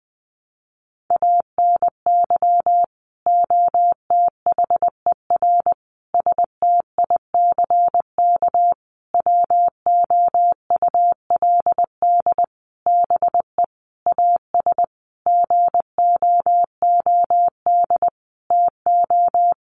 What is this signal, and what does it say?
Morse code, heard as a keyed tone, about 20 wpm, 700 Hz. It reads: ANY OTHER STICK WOULD BE AS GOOD TO
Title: Sentences from Top 1200 Words 20wpm